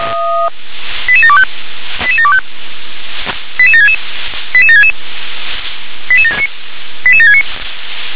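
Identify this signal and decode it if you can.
Five-tone selective call (ZVEI), heard as quick runs of five tones, tones of about 70 ms. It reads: 88725 88725 88766 88766 88769
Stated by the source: ZVEI five-tone selective call received by SDR.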